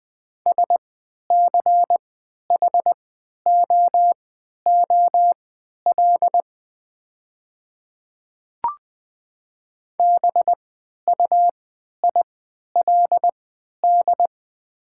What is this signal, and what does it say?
Morse code, heard as a keyed tone, about 20 wpm, spaced as for 11 wpm, 700 Hz. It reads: SCHOOL BUILD
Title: 3x Character Spacing - Top 300 Words - 20wpm